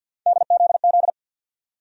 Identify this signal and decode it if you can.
Morse code, heard as a keyed tone, about 51 wpm, 700 Hz. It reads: DZZ
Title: USA Call Sign Postfixes - 3 Letters 50wpm